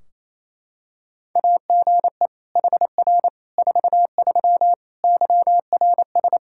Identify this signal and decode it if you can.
Morse code, heard as a keyed tone, about 28 wpm, 700 Hz. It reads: AGE HR 43 YRS
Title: Practice QSO - Exchange 3 - More Info - 28wpm